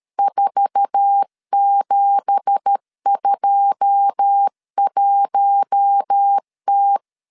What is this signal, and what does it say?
Morse code, heard as a keyed tone, about 13 wpm, 800 Hz. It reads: 4721T
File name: CW / DET / 1 9 1500C S C 12wpm 3811